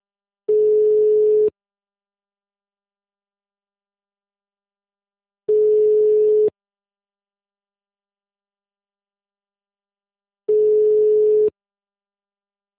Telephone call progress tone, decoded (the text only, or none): ringback tone